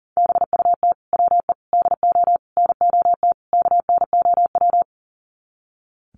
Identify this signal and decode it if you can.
Morse code, heard as a keyed tone, about 37 wpm, 700 Hz. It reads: BUT WE DO NOT KNOW
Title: Sets of 5 Words Encore2 40wpm